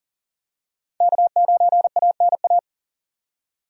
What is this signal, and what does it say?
Morse code, heard as a keyed tone, about 40 wpm, 700 Hz. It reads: K9ANA
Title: USA Call Signs - 5 Characters - 40wpm